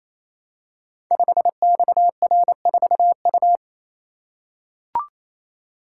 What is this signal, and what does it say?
Morse code, heard as a keyed tone, about 29 wpm, 700 Hz. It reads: 5XR4U
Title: Code Groups - 5 Letter:Numbers - 28wpm